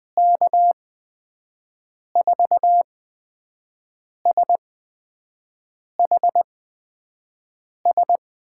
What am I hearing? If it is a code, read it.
Morse code, heard as a keyed tone, about 20 wpm, spaced as for 5 wpm, 700 Hz. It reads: K4SHS
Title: USA Call Signs - 6 Characters - 8x Character Spacing - 20wpm